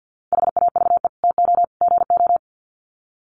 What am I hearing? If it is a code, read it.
Morse code, heard as a keyed tone, about 49 wpm, 700 Hz. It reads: HAVE TO GO
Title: Sets of 3 Words 50wpm